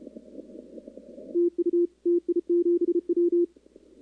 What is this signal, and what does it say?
Morse code, heard as a keyed tone, about 33 wpm, 340 Hz. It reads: TU TI7W
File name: contests / 2019-cw-wpx-kt5j-k5pi- -2330 0-11